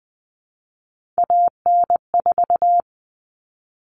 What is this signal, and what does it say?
Morse code, heard as a keyed tone, about 20 wpm, 700 Hz. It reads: AN4
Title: All Cut Numbers - 3-Digit 20wpm